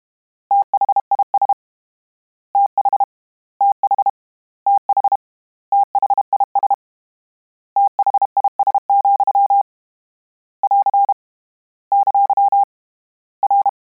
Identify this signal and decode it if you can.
Morse code, heard as a keyed tone, about 32 wpm, 800 Hz. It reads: THIS TH TH TH THIS THIS, <AR> ! R
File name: Combinations / Combinations-mp3s / Combinations@30